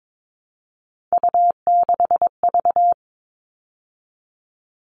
Morse code, heard as a keyed tone, about 22 wpm, 700 Hz. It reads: U6V
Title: All Cut Numbers - 3-Digit 22wpm